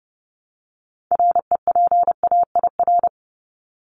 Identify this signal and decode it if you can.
Morse code, heard as a keyed tone, about 30 wpm, 700 Hz. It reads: REPAIR
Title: Top 1501-1750 Words 30wpm